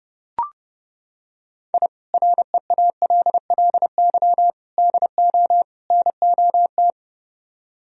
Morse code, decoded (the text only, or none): I REALLY DO NOT